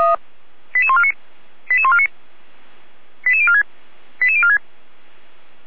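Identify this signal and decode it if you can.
Five-tone selective call (ZVEI), heard as quick runs of five tones, tones of about 70 ms. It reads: 88149 88149 88046 88046
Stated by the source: ZVEI five-tone selective call, SDR reception.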